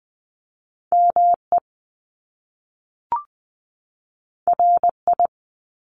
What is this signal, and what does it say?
Morse code, heard as a keyed tone, about 20 wpm, 700 Hz. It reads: ME RI
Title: Top Two Letter Combinations in Words 20wpm